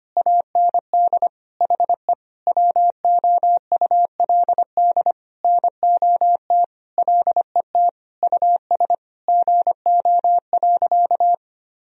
Morse code, heard as a keyed tone, about 25 wpm, 700 Hz. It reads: AND HE WOULD NOT LET US GO.
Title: Sentences from Top 1000 Words - Review Part 2 - 25wpm